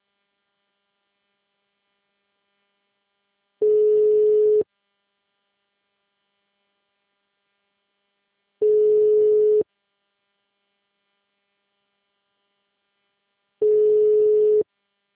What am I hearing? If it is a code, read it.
Ringback tone (Europe, 425 Hz).
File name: out- -2033- - - .61801